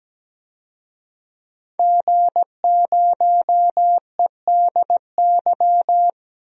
Morse code, heard as a keyed tone, about 17 wpm, 700 Hz. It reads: G0EDY
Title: DXCC-100 Call Signs - 5 Characters - 17wpm